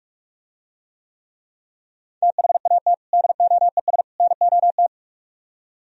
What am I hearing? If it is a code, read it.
Morse code, heard as a keyed tone, about 44 wpm, 700 Hz. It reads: THAT DOES NOT